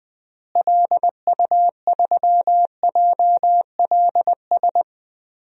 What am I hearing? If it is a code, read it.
Morse code, heard as a keyed tone, about 20 wpm, 700 Hz. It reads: LU3JLS
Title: DXCC-100 Call Signs - 6 Characters - Repeated 3x - 20wpm